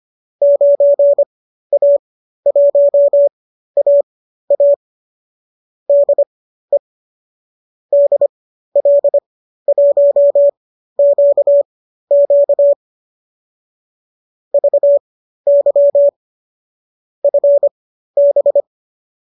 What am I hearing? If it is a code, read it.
Morse code, heard as a keyed tone, about 25 wpm, spaced as for 13 wpm, 590 Hz. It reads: 9A1AA DE DL1QQ VY FB